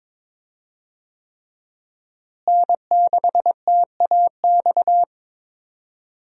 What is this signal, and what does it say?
Morse code, heard as a keyed tone, about 21 wpm, 700 Hz. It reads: N6TAX
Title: USA Call Signs - 5 Characters - Repeated 3x - 22wpm